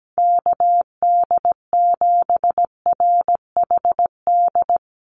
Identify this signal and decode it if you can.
Morse code, heard as a keyed tone, about 17 wpm, 700 Hz. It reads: KD7RHD
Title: USA Call Signs Repeated 3x 17wpm